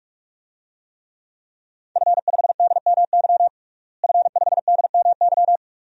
Morse code, heard as a keyed tone, about 46 wpm, 700 Hz. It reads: F5DMY F5DMY
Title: DXCC-100 Call Signs - 5 Characters - Repeated 2x - 45wpm